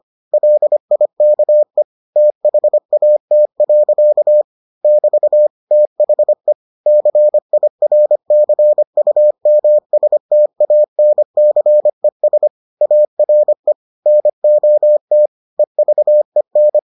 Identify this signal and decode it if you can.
Morse code, heard as a keyed tone, about 25 wpm, 600 Hz. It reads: LIKE THAT. <BT> THE CIRCUMSTANCES ARE NOT EVEN